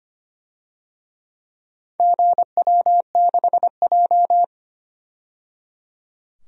Morse code, heard as a keyed tone, about 25 wpm, 700 Hz. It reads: GW6J
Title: DXCC-100 Call Signs - 4 Characters - 25wpm